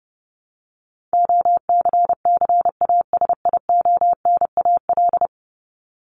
Morse code, heard as a keyed tone, about 30 wpm, 700 Hz. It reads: OCCASIONAL
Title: LY Suffix - Top 100 Words with Base Words 30wpm